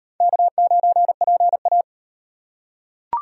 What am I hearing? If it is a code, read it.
Morse code, heard as a keyed tone, about 36 wpm, 700 Hz. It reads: K9PA